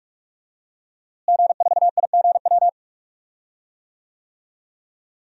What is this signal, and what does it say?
Morse code, heard as a keyed tone, about 42 wpm, 700 Hz. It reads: G4IGW
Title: DXCC-100 Call Signs - 5 Characters - Repeated 2x - 45wpm